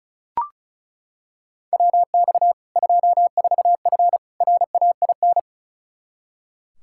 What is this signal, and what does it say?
Morse code, heard as a keyed tone, about 36 wpm, 700 Hz. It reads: WX 24F RAIN
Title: Practice QSO - Exchange 2 - Info - 35wpm